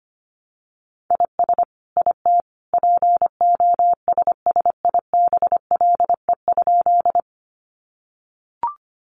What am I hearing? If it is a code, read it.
Morse code, heard as a keyed tone, about 25 wpm, 700 Hz. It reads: IS IT POSSIBLE?
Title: Sentences from Top 500 Words 25wpm